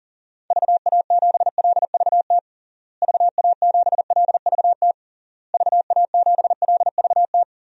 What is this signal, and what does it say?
Morse code, heard as a keyed tone, about 37 wpm, 700 Hz. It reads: VA7LVT VA7LVT VA7LVT
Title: DXCC-100 Call Signs - 6 Characters - Repeated 3x - 40wpm